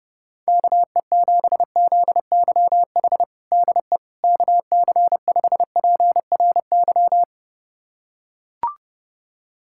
Morse code, heard as a keyed tone, about 32 wpm, 700 Hz. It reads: KE7ZYH DE KC5PRY
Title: USA Callsign DE Callsign - 6 Characters - 30wpm